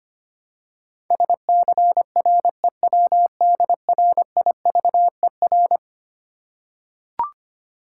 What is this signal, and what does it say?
Morse code, heard as a keyed tone, about 26 wpm, 690 Hz. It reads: SCREWDRIVER
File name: practice / 25wpm / Easy Words with Hints - 25wpm